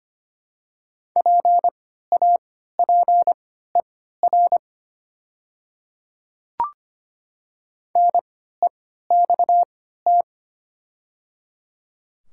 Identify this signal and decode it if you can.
Morse code, heard as a keyed tone, about 25 wpm, spaced as for 14 wpm, 700 Hz. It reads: PAPER NEXT
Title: 3x Character Spacing - Top 300 Words - 25wpm